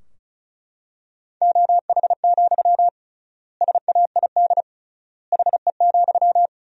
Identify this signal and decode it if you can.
Morse code, heard as a keyed tone, about 36 wpm, 700 Hz. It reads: OH, SAID HE,
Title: 3x Word-Spacing - Sentences from Top 500 Words - 35wpm